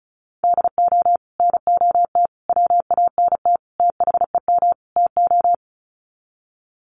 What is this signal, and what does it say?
Morse code, heard as a keyed tone, about 34 wpm, 700 Hz. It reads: DO NOT WANT THEM TO
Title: Sets of 5 Words Encore 35wpm